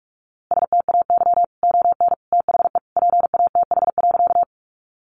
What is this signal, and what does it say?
Morse code, heard as a keyed tone, about 44 wpm, 700 Hz. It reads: STAY ON THE PATH.
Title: A Sound Of Thunder 45wpm